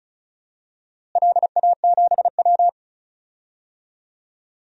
Morse code, heard as a keyed tone, about 36 wpm, 700 Hz. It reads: LA7W
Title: DXCC-100 Call Signs - 4 Characters - Repeated 3x - 35wpm